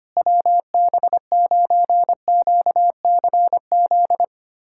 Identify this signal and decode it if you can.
Morse code, heard as a keyed tone, about 25 wpm, 700 Hz. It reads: WB9QCZ
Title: USA Callsign DE Callsign 25wpm